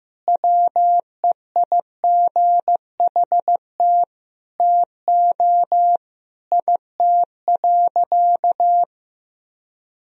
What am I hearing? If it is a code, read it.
Morse code, heard as a keyed tone, about 15 wpm, 700 Hz. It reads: WEIGHT TO IT.
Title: Sentences from Top 1000 Words - Review Part 3 - 15wpm